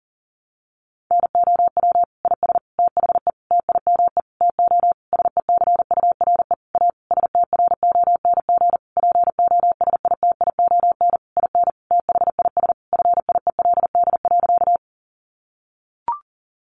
Morse code, heard as a keyed tone, about 40 wpm, 700 Hz. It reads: NOW IS THE TIME TO SECURE A STRONG POSITION IN THIS FIELD.
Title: Sentences from Top 1200 Words 40wpm